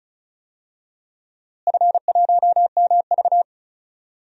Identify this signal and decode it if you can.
Morse code, heard as a keyed tone, about 36 wpm, 690 Hz. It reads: F1MV
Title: DXCC-100 Call Signs - 4 Characters - Repeated 3x - 35wpm